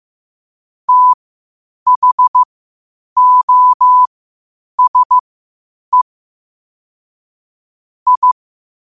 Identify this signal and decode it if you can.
Morse code, heard as a keyed tone, about 15 wpm, spaced as for 8 wpm, 1000 Hz. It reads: THOSE I
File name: qotdfiles / 8wpm